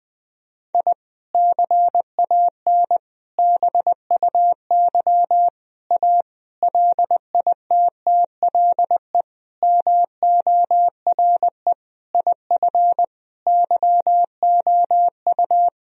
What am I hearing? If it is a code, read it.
Morse code, heard as a keyed tone, about 20 wpm, 700 Hz. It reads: I CAN BUY A LITTLE MORE IF YOU
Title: Sentences from Top 1000 Words - Review Part 1 - 20wpm